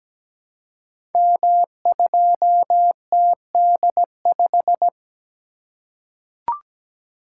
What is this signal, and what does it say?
Morse code, heard as a keyed tone, about 17 wpm, 700 Hz. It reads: M2TD5
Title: Code Groups - 5 Letter:Numbers - 17wpm